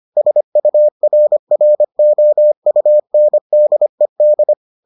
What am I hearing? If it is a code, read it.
Morse code, heard as a keyed tone, about 25 wpm, 600 Hz. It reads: SURROUNDED